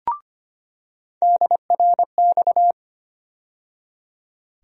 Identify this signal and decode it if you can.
Morse code, heard as a keyed tone, about 25 wpm, 700 Hz. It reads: DRX
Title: USA Call Sign Postfixes - 3 Letters 25wpm